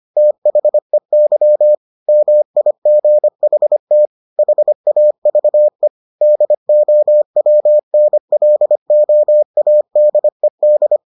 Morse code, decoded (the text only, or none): THEY MIGHT HAVE DOWNLOADED